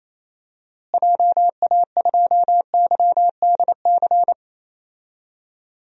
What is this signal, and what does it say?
Morse code, heard as a keyed tone, about 27 wpm, 700 Hz. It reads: JA2YDC
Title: DXCC-100 Call Signs - 6 Characters - Repeated 3x - 28wpm